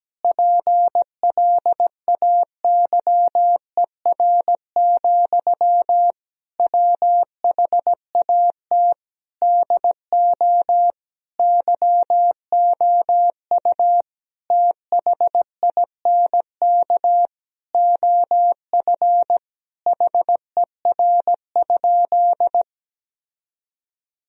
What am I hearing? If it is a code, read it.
Morse code, heard as a keyed tone, about 17 wpm, 700 Hz. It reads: PLAYER, WHAT DO YOU THINK OF HER?
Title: Sentences from Top 1000 Words - Review Part 4 - 17wpm